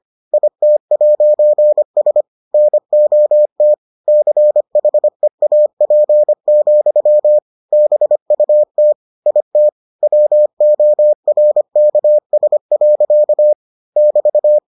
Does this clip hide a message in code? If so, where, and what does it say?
Yes: Morse code from 0.3 to 14.6 s, heard as a keyed tone, about 25 wpm, 600 Hz. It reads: IT'S NOT CHEAP, BUT IT WORKS. <BT>